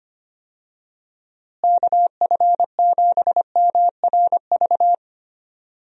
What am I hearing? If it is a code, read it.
Morse code, heard as a keyed tone, about 25 wpm, 700 Hz. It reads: KF7MRV
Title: USA Call Signs - 6 Characters - Repeated 2x - 25wpm